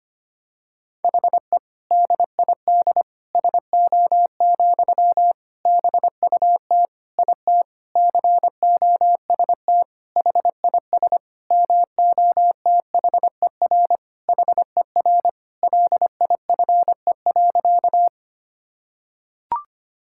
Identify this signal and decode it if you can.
Morse code, heard as a keyed tone, about 25 wpm, 700 Hz. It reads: HE DID SO, BUT IT COST HIS MOTHER HER LIFE.
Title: Sentences from Top 1000 Words - Review Part 5 - 25wpm